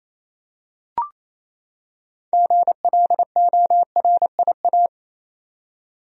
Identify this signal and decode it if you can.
Morse code, heard as a keyed tone, about 28 wpm, 700 Hz. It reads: GLORIA
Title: USA Common Names - 28wpm